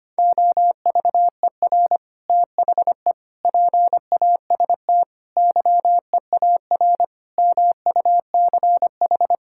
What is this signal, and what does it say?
Morse code, heard as a keyed tone, about 25 wpm, 700 Hz. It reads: OVER THE PAST YEAR MUCH
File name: practice / 25wpm / Sentences with 3-Word Phrases - 25wpm